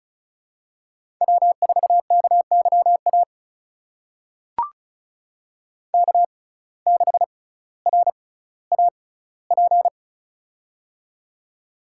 Morse code, heard as a keyed tone, about 36 wpm, spaced as for 18 wpm, 700 Hz. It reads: W4KYA K6RAP
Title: USA Call Signs - 5 Characters - 6x Character Spacing - 35wpm